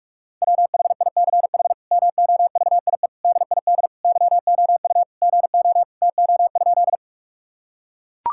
Morse code, heard as a keyed tone, about 43 wpm, 700 Hz. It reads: WHICH MOVIE DID YOU GO TO?
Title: Sentences from Top 1000 Words - Review Part 5 - 45wpm